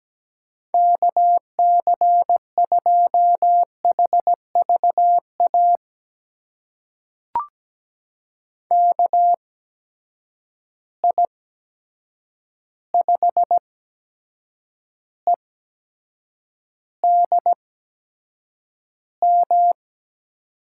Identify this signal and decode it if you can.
Morse code, heard as a keyed tone, about 17 wpm, spaced as for 7 wpm, 700 Hz. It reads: KC2HVA KI5EDM
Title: USA Call Signs - 6 Characters - 8x Character Spacing - 17wpm